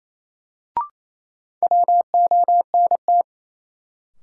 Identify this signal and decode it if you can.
Morse code, heard as a keyed tone, about 28 wpm, 700 Hz. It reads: WONT
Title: Sentences from Top 1000 Words - Review Part 6 - New Word Drill - 28wpm